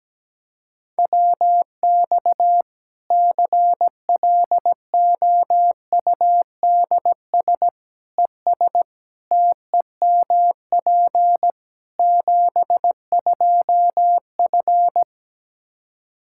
Morse code, heard as a keyed tone, about 17 wpm, 700 Hz. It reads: WX CLOUDS ES TEMP 72F
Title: Practice QSO - Exchange 2 - Info - 17wpm